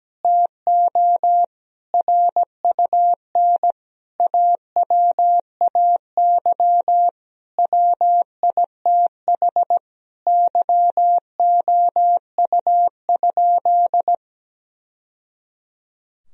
Morse code, heard as a keyed tone, about 17 wpm, 700 Hz. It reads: TO RUN AWAY WITH YOU?
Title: Sentences from Top 1000 Words - Review Part 3 - 17wpm